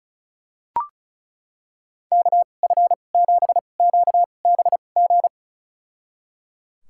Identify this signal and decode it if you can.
Morse code, heard as a keyed tone, about 36 wpm, spaced as for 25 wpm, 700 Hz. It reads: KF7QBG
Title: USA Call Signs - 6 Characters - 2x Character Spacing - 35wpm